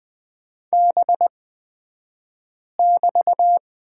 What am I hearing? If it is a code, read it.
Morse code, heard as a keyed tone, about 20 wpm, spaced as for 5 wpm, 700 Hz. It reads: B<BT>
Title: 223 Lesson 36d - BT - Two Characters 20wpm